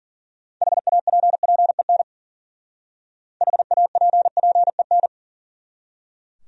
Morse code, heard as a keyed tone, about 49 wpm, 700 Hz. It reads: HAPPEN HAPPEN
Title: Speed-Racing Top 500 Words 40wpm